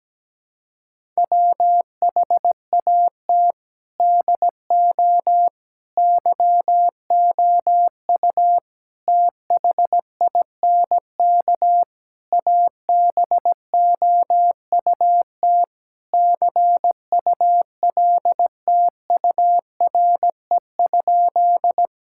Morse code, heard as a keyed tone, about 17 wpm, 700 Hz. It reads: WHAT DO YOU THINK ABOUT CULTURE?
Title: Sentences from Top 1000 Words - Review Part 6 - 17wpm